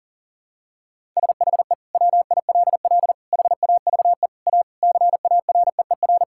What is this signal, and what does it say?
Morse code, heard as a keyed tone, about 40 wpm, 700 Hz. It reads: SHE WILL HAVE A CAREER
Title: Sentences from Top 1000 Words - Review Part 6 - 40wpm